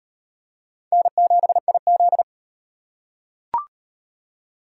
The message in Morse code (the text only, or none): N7IZ